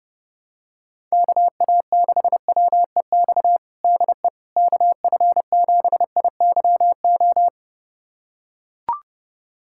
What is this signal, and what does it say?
Morse code, heard as a keyed tone, about 32 wpm, 700 Hz. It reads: KA6WEX DE KF7IYO
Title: USA Callsign DE Callsign - 6 Characters - 30wpm